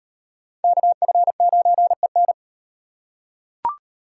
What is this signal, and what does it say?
Morse code, heard as a keyed tone, about 38 wpm, 700 Hz. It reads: KF9EN